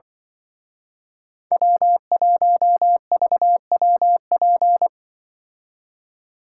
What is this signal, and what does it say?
Morse code, heard as a keyed tone, about 24 wpm, 700 Hz. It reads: W1VWP
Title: Speed Racing - USA 1x3 Call Signs 20wpm